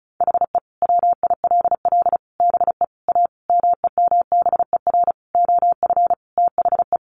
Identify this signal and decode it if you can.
Morse code, heard as a keyed tone, about 34 wpm, 700 Hz. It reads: HE WILL BE A MEMBER OF THE
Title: Sentences from Top 1100 Words 35wpm